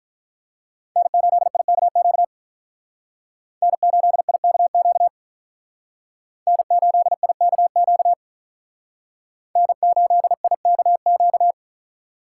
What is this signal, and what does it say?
Morse code, heard as a keyed tone, about 43 wpm, 700 Hz. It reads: N8IKQ N8IKQ N8IKQ N8IKQ
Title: Speed Racing - USA 1x3 Call Signs 35wpm